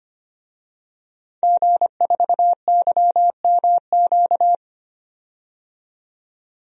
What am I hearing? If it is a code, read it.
Morse code, heard as a keyed tone, about 25 wpm, 700 Hz. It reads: G4YMQ
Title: DXCC-100 Call Signs - 5 Characters - 25wpm